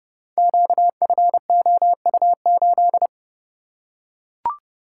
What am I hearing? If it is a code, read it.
Morse code, heard as a keyed tone, about 28 wpm, 700 Hz. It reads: QFOU8
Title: Code Groups - 5 Letter:Numbers - 30wpm